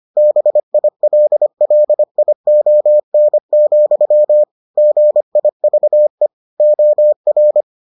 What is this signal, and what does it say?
Morse code, heard as a keyed tone, about 25 wpm, 600 Hz. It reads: BILLION, GIVE OR